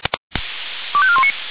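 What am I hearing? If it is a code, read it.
Five-tone selective call (ZVEI), heard as quick runs of five tones, tones of about 70 ms. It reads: 25419
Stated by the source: ZVEI five-tone selective call received with an SDR.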